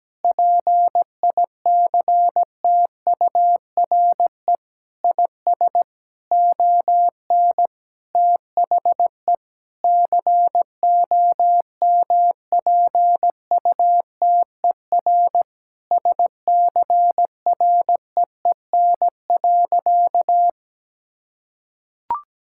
Morse code, heard as a keyed tone, about 17 wpm, 700 Hz. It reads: PICTURE IS ON THE COMPUTER SCREEN.
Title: Sentences from Top 1000 Words - Review Part 6 - 17wpm